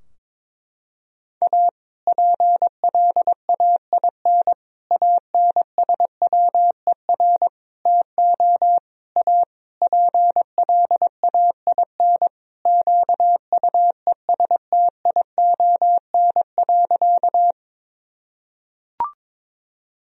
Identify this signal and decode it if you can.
Morse code, heard as a keyed tone, about 22 wpm, 700 Hz. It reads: A PLAIN ANSWER TO A PLAIN QUESTION.
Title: Sentences from Top 1000 Words - Review Part 2 - 22wpm